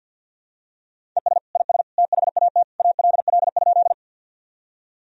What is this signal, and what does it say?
Morse code, heard as a keyed tone, about 52 wpm, 700 Hz. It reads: ES IS THAT ALL?
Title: Sentences from Top 100 Words and Abbreviations 50wpm